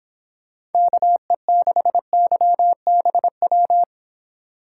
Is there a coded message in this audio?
Yes: Morse code from 0.7 to 3.8 s, heard as a keyed tone, about 24 wpm, 700 Hz. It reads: KE6YBW